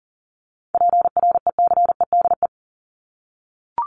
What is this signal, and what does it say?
Morse code, heard as a keyed tone, about 40 wpm, 700 Hz. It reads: PRECEDE
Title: ING Suffix - Top 100 Words with Base Words 40wpm